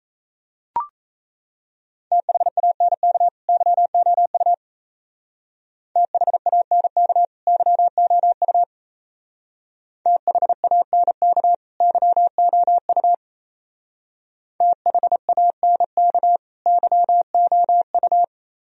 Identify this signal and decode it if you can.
Morse code, heard as a keyed tone, about 34 wpm, 700 Hz. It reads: THANK YOU THANK YOU THANK YOU THANK YOU
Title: Speed-Racing - Sets of 2 Words 28wpm